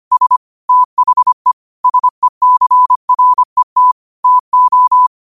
Morse code, heard as a keyed tone, about 25 wpm, 1000 Hz. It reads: S THE SECRET TO
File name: qotdfiles / 25wpm